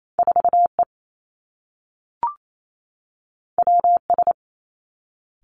Morse code, heard as a keyed tone, about 28 wpm, 700 Hz. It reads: 4 E W S
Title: Two Challenging Letter-Number Combinations 28wpm